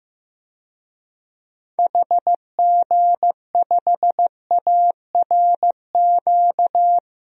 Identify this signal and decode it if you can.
Morse code, heard as a keyed tone, about 15 wpm, 700 Hz. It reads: HG5ARQ
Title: DXCC-100 Call Signs - 6 Characters - Repeated 2x - 15wpm